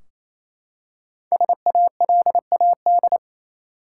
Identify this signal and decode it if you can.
Morse code, heard as a keyed tone, about 28 wpm, 700 Hz. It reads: SALAD
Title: Five Letter Words - 28wpm